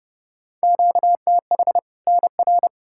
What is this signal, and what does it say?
Morse code, heard as a keyed tone, about 30 wpm, 700 Hz. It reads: QTH NR
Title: Practice QSO - Exchange 1 - Beginning - 30wpm